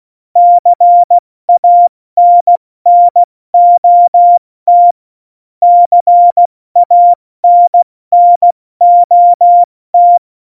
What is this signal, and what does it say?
Morse code, heard as a keyed tone, about 16 wpm, spaced as for 14 wpm, 700 Hz. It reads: CANNOT CANNOT
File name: Morsekurs / Witze / Witz 25 Juli 17, Tempo 70 F80